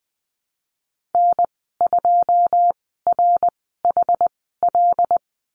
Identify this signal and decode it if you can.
Morse code, heard as a keyed tone, about 20 wpm, spaced as for 15 wpm, 700 Hz. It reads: N2RHL
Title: 088 Lesson 15g - Number 5 - Call Signs 20wpm